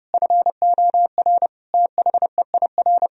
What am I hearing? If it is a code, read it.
Morse code, heard as a keyed tone, about 30 wpm, 700 Hz. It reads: FOR THEIR
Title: Sentences from Top 1000 Words - Review Part 4 - 30wpm